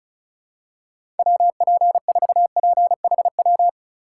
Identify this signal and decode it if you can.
Morse code, heard as a keyed tone, about 34 wpm, 700 Hz. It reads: WP4PHW
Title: USA Call Signs - 6 Characters - 4x Character Spacing - 35wpm